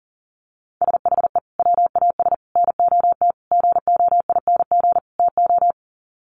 Morse code, heard as a keyed tone, about 40 wpm, 700 Hz. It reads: SHE WAS NOT GOING TO
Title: Sets of 5 Words Encore2 40wpm